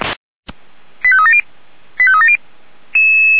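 Five-tone selective call (ZVEI), heard as quick runs of five tones, tones of about 70 ms. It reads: 85370 85370
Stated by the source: ZVEI five-tone selective call 89200 received by SDR, fire brigade feuerwehr-orlamuende.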